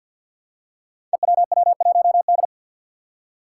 Morse code, heard as a keyed tone, about 52 wpm, 700 Hz. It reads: EWW1D
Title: Five Challenging Letter-Number Combinations 50wpm